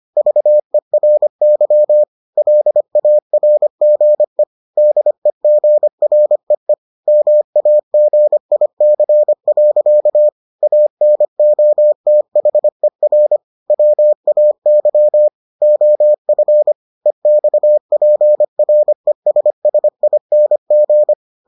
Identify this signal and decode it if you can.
Morse code, heard as a keyed tone, about 25 wpm, 600 Hz. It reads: VERY LARGE DEGREE MAGIC. ANOTHER WAY OF EXPRESSING